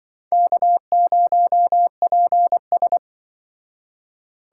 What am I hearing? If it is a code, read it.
Morse code, heard as a keyed tone, about 25 wpm, 700 Hz. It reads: K0PS
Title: Speed Racing - USA 1x2 Call Signs 20wpm